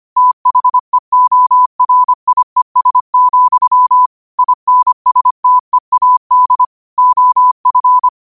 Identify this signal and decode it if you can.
Morse code, heard as a keyed tone, about 25 wpm, 1000 Hz. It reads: THEORIES, INSTEAD OF